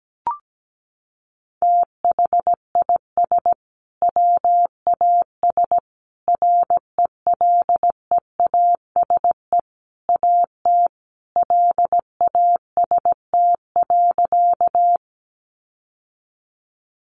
Morse code, heard as a keyed tone, about 17 wpm, 700 Hz. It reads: THIS WAS RELEASE AT LAST.
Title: Sentences from Top 1100 Words 17wpm